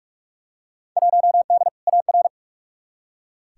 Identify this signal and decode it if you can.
Morse code, heard as a keyed tone, about 46 wpm, 700 Hz. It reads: 1D AR
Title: Field Day Exchanges - 45wpm